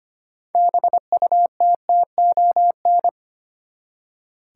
Morse code, heard as a keyed tone, about 25 wpm, 700 Hz. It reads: BUTTON
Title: Six Letter Words - 25wpm